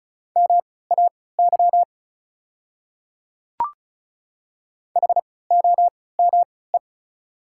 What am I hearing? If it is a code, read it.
Morse code, heard as a keyed tone, about 36 wpm, spaced as for 20 wpm, 690 Hz. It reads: MAY HOME
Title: 3x Character Spacing - Top 200 Words - 35wpm